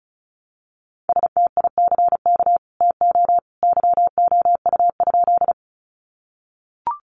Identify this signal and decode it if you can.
Morse code, heard as a keyed tone, about 34 wpm, 700 Hz. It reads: STICK TO YOU?
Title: Sentences from Top 1200 Words 35wpm